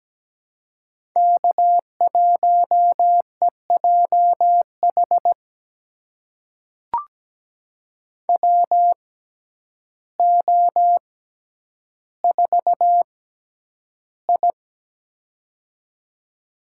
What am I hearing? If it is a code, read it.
Morse code, heard as a keyed tone, about 17 wpm, spaced as for 9 wpm, 700 Hz. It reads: K1EJH WO4I
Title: USA Call Signs - 6x Character Spacing - 17wpm